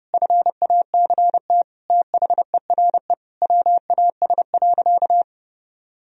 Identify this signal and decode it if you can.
Morse code, heard as a keyed tone, about 29 wpm, 700 Hz. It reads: FACT THERE WAS.
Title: Sentences from Top 1000 Words - Review Part 3 - 30wpm